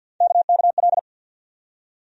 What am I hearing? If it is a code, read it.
Morse code, heard as a keyed tone, about 49 wpm, 700 Hz. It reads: KKL
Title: USA Call Sign Postfixes - 3 Letters 50wpm